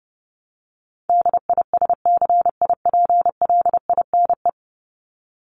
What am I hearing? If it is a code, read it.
Morse code, heard as a keyed tone, about 30 wpm, 700 Hz. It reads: DISCIPLINE
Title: Top 751-1000 Words 30wpm